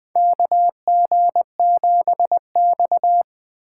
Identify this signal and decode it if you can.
Morse code, heard as a keyed tone, about 20 wpm, 700 Hz. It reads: KG7X
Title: USA Call Signs - 4 Characters - 4x Character Spacing - 20wpm